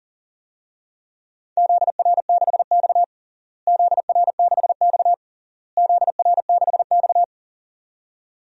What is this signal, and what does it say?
Morse code, heard as a keyed tone, about 40 wpm, 700 Hz. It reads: ZR6X ZR6X ZR6X
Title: DXCC-100 Call Signs - 4 Characters - Repeated 3x - 40wpm